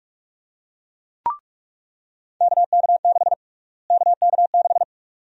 Morse code, heard as a keyed tone, about 46 wpm, 700 Hz. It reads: KK6 KK6
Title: USA Call Sign Prefixes - Repeated 2x - 45wpm